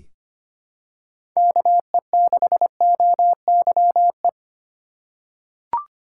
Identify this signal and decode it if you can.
Morse code, heard as a keyed tone, about 25 wpm, 700 Hz. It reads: KE6OYE
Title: USA Call Signs - 4x Character Spacing - 25wpm